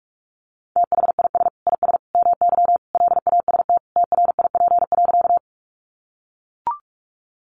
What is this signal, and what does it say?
Morse code, heard as a keyed tone, about 44 wpm, 700 Hz. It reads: THIS IS MY LAST TRIP.
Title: Sentences from Top 1300 Words 45wpm